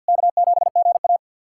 Morse code, heard as a keyed tone, about 52 wpm, 700 Hz. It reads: K7GA
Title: K1USN SST QSO Practice - 50wpm